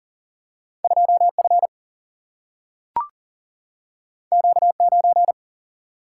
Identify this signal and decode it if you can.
Morse code, heard as a keyed tone, about 40 wpm, 700 Hz. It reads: 2 F Q 9
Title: Single-Letter Single-Number 40wpm